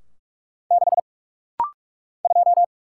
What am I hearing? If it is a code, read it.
Morse code, heard as a keyed tone, about 44 wpm, 690 Hz. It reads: 6 2
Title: Single Letter-Number - Rapid-Fire 45wpm